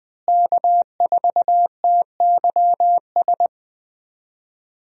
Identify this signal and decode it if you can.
Morse code, heard as a keyed tone, about 20 wpm, 700 Hz. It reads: K4TYS
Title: USA Call Signs - 5 Characters - Repeated 2x - 20wpm